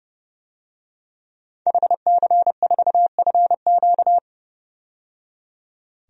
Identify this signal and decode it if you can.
Morse code, heard as a keyed tone, about 30 wpm, 700 Hz. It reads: HC4FQ
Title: DXCC-100 Call Signs - 5 Characters - 30wpm